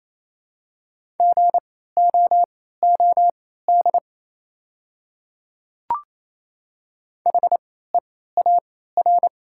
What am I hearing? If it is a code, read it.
Morse code, heard as a keyed tone, about 28 wpm, spaced as for 16 wpm, 700 Hz. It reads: GOOD HEAR
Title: 3x Character Spacing - Top 300 Words - 28wpm